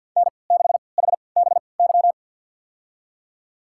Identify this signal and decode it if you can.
Morse code, heard as a keyed tone, about 46 wpm, spaced as for 27 wpm, 700 Hz. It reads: N6HBY
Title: USA Call Signs - 3x Character Spacing - 50wpm